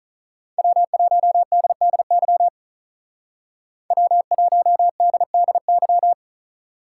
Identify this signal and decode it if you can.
Morse code, heard as a keyed tone, about 46 wpm, 700 Hz. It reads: W1DDY W1DDY